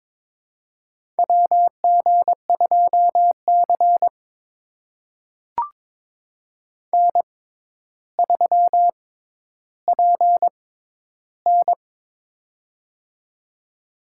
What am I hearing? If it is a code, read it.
Morse code, heard as a keyed tone, about 22 wpm, spaced as for 11 wpm, 700 Hz. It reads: WG2C N3PN
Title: USA Call Signs - 4 Characters - 6x Character Spacing - 22wpm